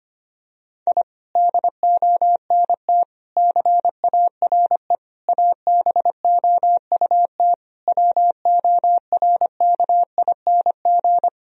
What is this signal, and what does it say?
Morse code, heard as a keyed tone, about 25 wpm, 700 Hz. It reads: I DONT CARE ABOUT WORKING
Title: Sentences from Top 1000 Words - Review Part 4 - 25wpm